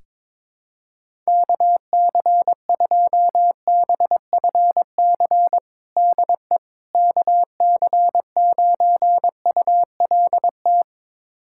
Morse code, heard as a keyed tone, about 22 wpm, 700 Hz. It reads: KC2BFC DE KC9ULT
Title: USA Callsign DE Callsign - 6 Characters - 22wpm